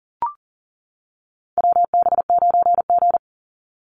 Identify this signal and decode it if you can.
Morse code, heard as a keyed tone, about 40 wpm, 700 Hz. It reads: WB9G
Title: Call Signs - 40wpm